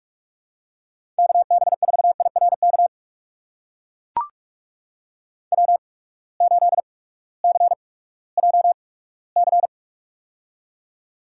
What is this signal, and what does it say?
Morse code, heard as a keyed tone, about 42 wpm, spaced as for 20 wpm, 700 Hz. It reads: KB4IRK W8CJC
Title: USA Call Signs - 8x Character Spacing - 45wpm